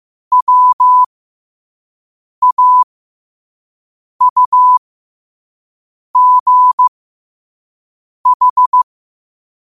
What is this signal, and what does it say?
Morse code, heard as a keyed tone, about 15 wpm, spaced as for 5 wpm, 1000 Hz. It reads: WAUGH
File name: qotdfiles / 5wpm